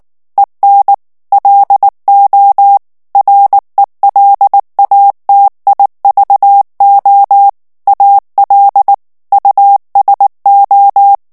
Morse code, heard as a keyed tone, about 19 wpm, 800 Hz. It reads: EN LO RELATIVO AL USO
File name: morse mp3 / ESPAGNOL TEXTE EN CLAIR 2 VITESSE 18 MOTS MINUTES